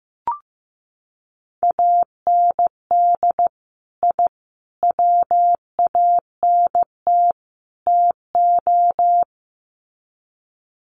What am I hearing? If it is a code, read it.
Morse code, heard as a keyed tone, about 15 wpm, 700 Hz. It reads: AND I WANT TO
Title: Sets of 4 Words 15wpm